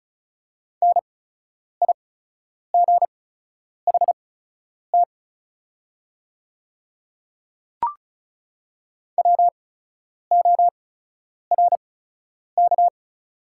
Morse code, heard as a keyed tone, about 36 wpm, spaced as for 10 wpm, 700 Hz. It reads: NIGHT WORK
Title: 8x Character Spacing - Top 300 Words - 35wpm